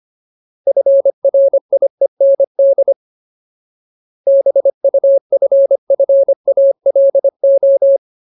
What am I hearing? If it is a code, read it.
Morse code, heard as a keyed tone, about 25 wpm, 550 Hz. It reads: FRIEND BUFFALO
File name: Ultimate Word List / 25WPM Ultimate